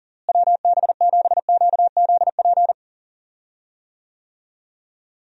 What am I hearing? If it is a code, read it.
Morse code, heard as a keyed tone, about 40 wpm, 700 Hz. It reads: WB7QZP
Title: USA Call Signs - 6 Characters - Repeated 2x - 40wpm